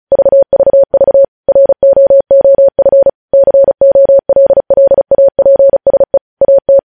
Morse code, heard as a keyed tone, about 36 wpm, 550 Hz. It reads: VVV ROOF COLLAPSE AT
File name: podcast / News headlines-1x-35x35